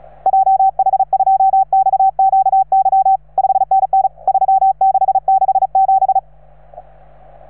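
Morse code, heard as a keyed tone, about 35 wpm, 770 Hz. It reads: JH2XQY 5NN 3667
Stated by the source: Morse from station 3V9A.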